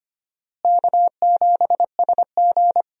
Morse code, heard as a keyed tone, about 25 wpm, 700 Hz. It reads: K7SG
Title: USA Call Signs - 4 Characters - Repeated 3x - 25wpm